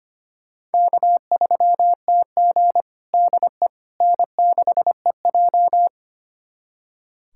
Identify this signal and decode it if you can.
Morse code, heard as a keyed tone, about 25 wpm, 700 Hz. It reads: K3TG DE N6EJ
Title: USA Callsign DE Callsign - 4 Characters - 25wpm